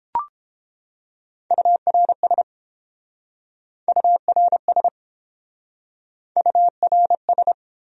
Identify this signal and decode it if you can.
Morse code, heard as a keyed tone, about 29 wpm, 700 Hz. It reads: URS URS URS